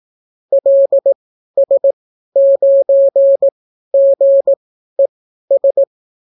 Morse code, heard as a keyed tone, about 18 wpm, spaced as for 12 wpm, 550 Hz. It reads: LS9GES